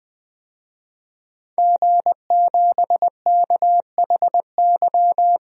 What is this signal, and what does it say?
Morse code, heard as a keyed tone, about 20 wpm, 700 Hz. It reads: G7KHY
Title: DXCC-100 Call Signs - 5 Characters - 20wpm